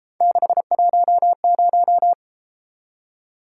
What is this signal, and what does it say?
Morse code, heard as a keyed tone, about 31 wpm, 700 Hz. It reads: 610